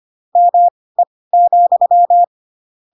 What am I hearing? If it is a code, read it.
Morse code, heard as a keyed tone, about 25 wpm, spaced as for 18 wpm, 700 Hz. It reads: ME,
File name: Pinocchio / Pinocchio ch4 12 25 ik1hge